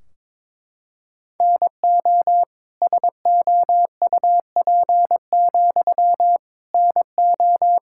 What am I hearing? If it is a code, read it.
Morse code, heard as a keyed tone, about 23 wpm, 700 Hz. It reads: NO SOUP, NO
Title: Sentences from Top 1600 Words 22wpm